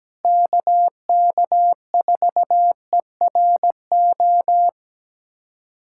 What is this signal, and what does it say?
Morse code, heard as a keyed tone, about 17 wpm, 700 Hz. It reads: KK4ERO
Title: USA Callsign DE Callsign 17wpm